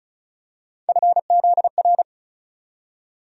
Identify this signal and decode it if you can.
Morse code, heard as a keyed tone, about 36 wpm, 700 Hz. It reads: FZR
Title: USA Call Sign Postfixes - 3 Letters 35wpm